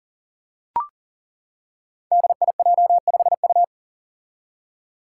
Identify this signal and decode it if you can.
Morse code, heard as a keyed tone, about 38 wpm, 700 Hz. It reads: DIJ5U
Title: Five Challenging Letter-Number Combinations 40wpm